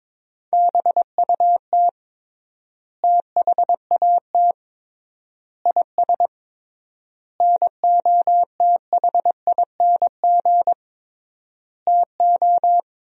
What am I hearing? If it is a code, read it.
Morse code, heard as a keyed tone, about 22 wpm, 700 Hz. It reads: BUT THAT IS NOTHING TO
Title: 3x Word-Spacing - Sentences from Top 500 Words - 22wpm